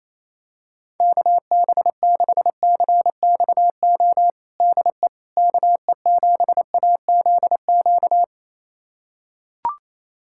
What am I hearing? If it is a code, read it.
Morse code, heard as a keyed tone, about 28 wpm, 700 Hz. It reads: KB6CXO DE KE7AZQ
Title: USA Callsign DE Callsign - 6 Characters - 28wpm